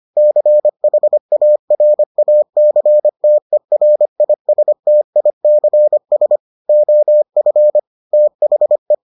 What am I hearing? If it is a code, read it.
Morse code, heard as a keyed tone, about 25 wpm, 600 Hz. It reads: CHARACTERISTICS OF THE